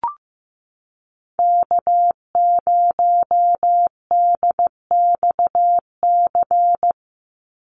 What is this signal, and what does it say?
Morse code, heard as a keyed tone, about 15 wpm, 700 Hz. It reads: K0DXC
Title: Call Signs Repeated 2x - 15wpm